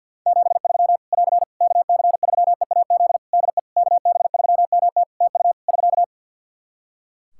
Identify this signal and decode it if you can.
Morse code, heard as a keyed tone, about 47 wpm, 700 Hz. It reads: 73 <AR> KC3EAZ DE KB3MT TU <SK>
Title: Practice QSO - Exchange 4 - Ending - 50wpm